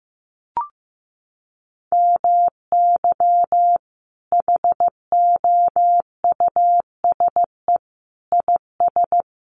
Morse code, heard as a keyed tone, about 15 wpm, 700 Hz. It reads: MY HOUSE IS
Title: Sentences from Top 1100 Words 15wpm